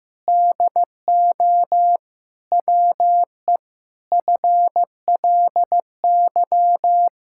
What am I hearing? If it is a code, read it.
Morse code, heard as a keyed tone, about 15 wpm, 700 Hz. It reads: DO WE FLY
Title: Pangram Challenge v2 - 15wpm